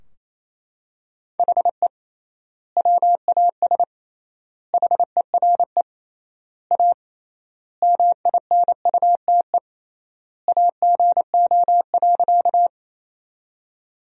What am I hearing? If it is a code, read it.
Morse code, heard as a keyed tone, about 28 wpm, 700 Hz. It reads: HE WAS HERE A MINUTE AGO.
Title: 3x Word-Spacing - Sentences from Top 500 Words - 28wpm